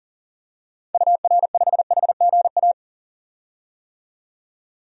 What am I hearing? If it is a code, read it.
Morse code, heard as a keyed tone, about 40 wpm, 700 Hz. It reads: UR5HGA
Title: DXCC-100 Call Signs - 6 Characters - 40wpm